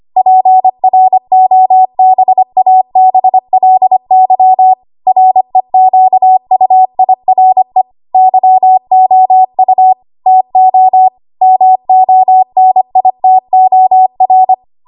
Morse code, heard as a keyed tone, about 25 wpm, 750 Hz. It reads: PROBABLY REQUIRE YOU TO MONITOR
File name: Archive / 25 WPM / 25WPM